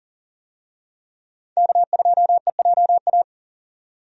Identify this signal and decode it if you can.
Morse code, heard as a keyed tone, about 38 wpm, 700 Hz. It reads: K2EJA
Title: USA Call Signs - 5 Characters - 40wpm